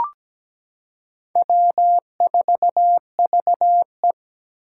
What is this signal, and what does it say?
Morse code, heard as a keyed tone, about 17 wpm, 700 Hz. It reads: W4VE
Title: Four Challenging Letter-Number Combinations 17wpm